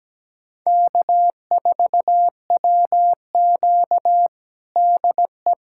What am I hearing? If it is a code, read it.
Morse code, heard as a keyed tone, about 17 wpm, 700 Hz. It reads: K4WQ DE
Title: USA Callsign DE Callsign 17wpm